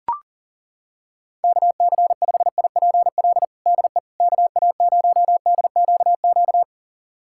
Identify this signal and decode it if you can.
Morse code, heard as a keyed tone, about 37 wpm, 700 Hz. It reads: KC5IPL DE KA0DQQ
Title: USA Callsign DE Callsign - 6 Characters - 40wpm